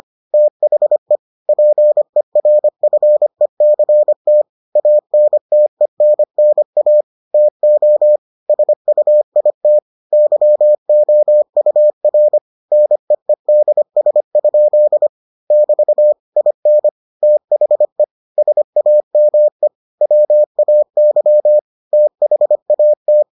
Morse code, heard as a keyed tone, about 25 wpm, 600 Hz. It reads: THE PERFECT ANTENNA TO SUIT YOUR NEEDS? <BT> IN THE SAME WAY THAT